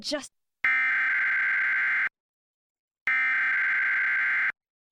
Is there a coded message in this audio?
Yes: Emergency Alert System (SAME) tones (header bursts) from 0.6 to 4.5 s, heard as harsh buzzing bursts, with speech.